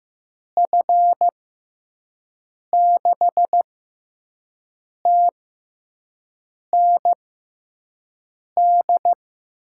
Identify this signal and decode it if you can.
Morse code, heard as a keyed tone, about 15 wpm, spaced as for 5 wpm, 700 Hz. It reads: F6TND
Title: USA Call Signs - 6 Characters - 6x Character Spacing - 15wpm